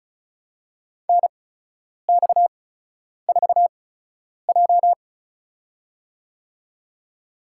Morse code, heard as a keyed tone, about 36 wpm, spaced as for 10 wpm, 700 Hz. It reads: NX4J
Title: USA Call Signs - 4 Characters - 8x Character Spacing - 35wpm